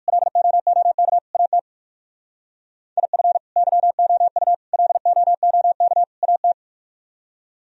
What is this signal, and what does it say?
Morse code, heard as a keyed tone, about 54 wpm, 700 Hz. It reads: LOOK AT IF YOU LOOK AT
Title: Speed-Racing - Sets of 4 Words 45wpm